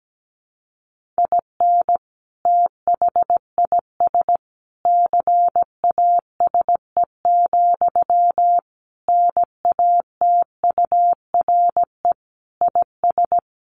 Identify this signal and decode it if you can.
Morse code, heard as a keyed tone, about 17 wpm, 700 Hz. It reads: IN THIS CASE, NATURE IS
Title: Sentences from Top 1400 Words 17wpm